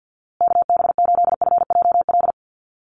Morse code, heard as a keyed tone, about 49 wpm, 700 Hz. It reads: KB8FJL
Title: USA Call Signs Repeated 3x 50wpm